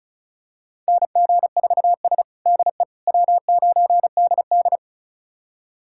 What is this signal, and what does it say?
Morse code, heard as a keyed tone, about 36 wpm, 700 Hz. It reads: NG4S DE W9DD
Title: USA Callsign DE Callsign - 4 Characters - 35wpm